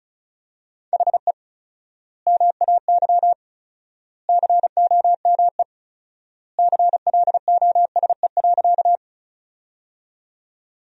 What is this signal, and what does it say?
Morse code, heard as a keyed tone, about 36 wpm, 700 Hz. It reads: HE MAY COME CLOSE.
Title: 4x Word-Spacing - Sentences from Top 300 Words - 35wpm